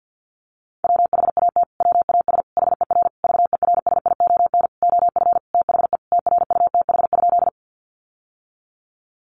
Morse code, heard as a keyed tone, about 48 wpm, 700 Hz. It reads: WHAT WAS HER VERSION OF THE TRUTH?
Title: Sentences from Top 1400 Words 50wpm